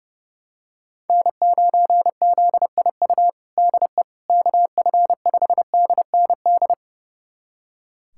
Morse code, heard as a keyed tone, about 30 wpm, 700 Hz. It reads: N9ZIU DE KF5DND
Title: USA Callsign DE Callsign 30wpm